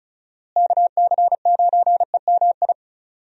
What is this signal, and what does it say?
Morse code, heard as a keyed tone, about 36 wpm, 700 Hz. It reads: KC9EMI
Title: USA Call Signs - 6 Characters - Repeated 2x - 35wpm